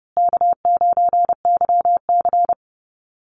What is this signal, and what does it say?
Morse code, heard as a keyed tone, about 30 wpm, 700 Hz. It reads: K9YC
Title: Call Signs Repeated 3x - 30wpm